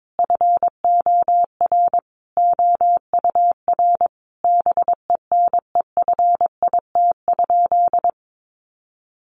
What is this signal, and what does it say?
Morse code, heard as a keyed tone, about 22 wpm, 700 Hz. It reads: FOR OUR BENEFIT?
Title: Sentences from Top 1500 Words 22wpm